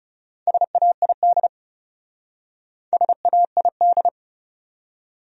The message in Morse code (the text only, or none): SAID SAID